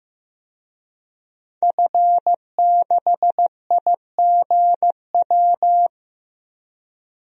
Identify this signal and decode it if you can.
Morse code, heard as a keyed tone, about 15 wpm, 700 Hz. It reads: F6IGW
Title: DXCC-100 Call Signs - 5 Characters - 15wpm